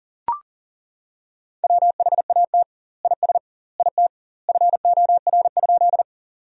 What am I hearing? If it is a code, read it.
Morse code, heard as a keyed tone, about 40 wpm, 700 Hz. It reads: WHAT IS IT FOR?
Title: 2x Word-Spacing - Sentences from Top 100 Words - 40wpm